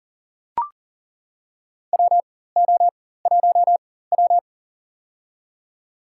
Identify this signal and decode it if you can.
Morse code, heard as a keyed tone, about 40 wpm, spaced as for 19 wpm, 700 Hz. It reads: WO1W
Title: USA Call Signs - 4 Characters - 4x Character Spacing - 40wpm